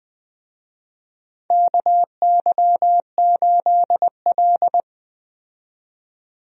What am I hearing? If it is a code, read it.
Morse code, heard as a keyed tone, about 20 wpm, 700 Hz. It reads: KY8L
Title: USA Call Signs - 4 Characters - 20wpm